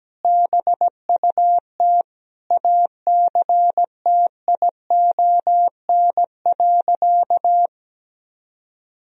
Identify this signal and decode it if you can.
Morse code, heard as a keyed tone, about 17 wpm, 700 Hz. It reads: BUT ACTION.
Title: Sentences from Top 1000 Words - Review Part 4 - 17wpm